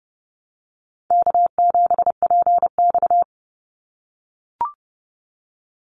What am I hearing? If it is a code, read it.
Morse code, heard as a keyed tone, about 30 wpm, 700 Hz. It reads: K7PX
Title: NA SOTA Activator Callsigns 30wpm